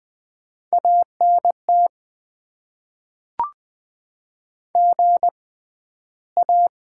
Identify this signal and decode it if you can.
Morse code, heard as a keyed tone, about 20 wpm, spaced as for 12 wpm, 700 Hz. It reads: ANT GA
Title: Top 20 QSO Elements - 6x Character Spacing - 20wpm